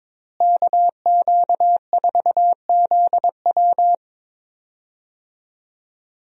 Morse code, heard as a keyed tone, about 22 wpm, 700 Hz. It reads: KQ4ZW
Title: USA Call Signs - 5 Characters - Repeated 2x - 22wpm